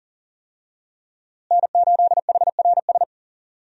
Morse code, heard as a keyed tone, about 39 wpm, 700 Hz. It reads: N8HRS
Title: USA Call Signs - 5 Characters - Repeated 3x - 40wpm